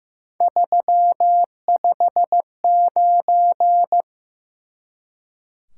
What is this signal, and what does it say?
Morse code, heard as a keyed tone, about 15 wpm, 700 Hz. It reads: 359
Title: Top 180 QSO Elements 15wpm